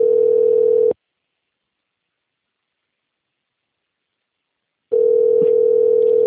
Ringback tone (North America), 440 + 480 Hz.
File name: rg-1600-unknown- - - .53912